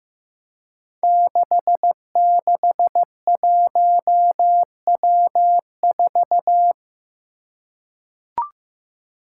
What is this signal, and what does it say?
Morse code, heard as a keyed tone, about 15 wpm, 700 Hz. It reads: 661W4
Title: Five Challenging Letter-Number Combinations 15wpm